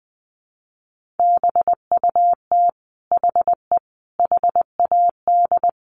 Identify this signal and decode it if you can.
Morse code, heard as a keyed tone, about 20 wpm, 700 Hz. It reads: BUT HE HAD
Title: Sentences from Top 1500 Words 20wpm